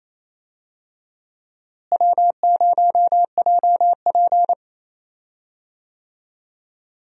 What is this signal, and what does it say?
Morse code, heard as a keyed tone, about 28 wpm, 700 Hz. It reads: W0JP
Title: USA Call Signs - 4 Characters - Repeated 2x - 28wpm